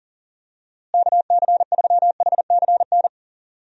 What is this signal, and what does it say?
Morse code, heard as a keyed tone, about 40 wpm, 700 Hz. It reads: KC3HCN
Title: USA Call Signs - 6 Characters - Repeated 2x - 40wpm